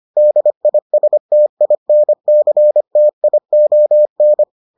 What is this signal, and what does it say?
Morse code, heard as a keyed tone, about 25 wpm, 600 Hz. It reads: DISTINCTION